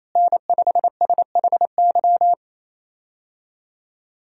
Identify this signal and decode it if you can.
Morse code, heard as a keyed tone, about 28 wpm, 700 Hz. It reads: N5SHY
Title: USA Call Signs - 5 Characters - Repeated 2x - 28wpm